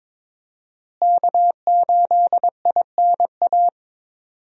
Morse code, heard as a keyed tone, about 22 wpm, 700 Hz. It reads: K8INA
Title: USA Call Signs - 5 Characters - Repeated 2x - 22wpm